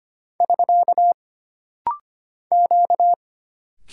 Morse code, heard as a keyed tone, about 25 wpm, 700 Hz. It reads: <SK> Q
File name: Single Character (Letter:Number:Punctuation:Prosign) - Rapid-Fire - 25wpm